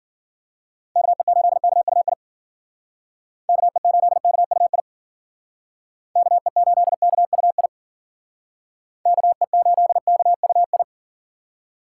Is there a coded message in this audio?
Yes: Morse code from 1.0 to 10.8 s, heard as a keyed tone, about 51 wpm, 700 Hz. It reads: KE8KUI KE8KUI KE8KUI KE8KUI